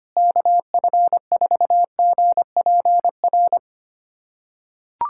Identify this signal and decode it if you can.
Morse code, heard as a keyed tone, about 25 wpm, 700 Hz. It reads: KF4GPR
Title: USA Callsign DE Callsign - 6 Characters - 25wpm